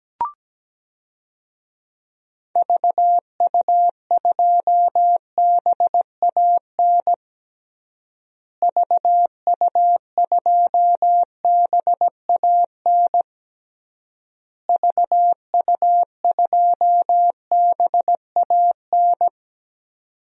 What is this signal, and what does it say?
Morse code, heard as a keyed tone, about 17 wpm, 700 Hz. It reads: VU2BAN VU2BAN VU2BAN
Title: DXCC-100 Call Signs - 6 Characters - Repeated 3x - 17wpm